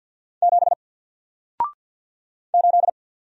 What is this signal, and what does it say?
Morse code, heard as a keyed tone, about 48 wpm, 700 Hz. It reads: 7 8
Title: Single Character (Letter:Number:Punctuation:Prosign) Alt 50wpm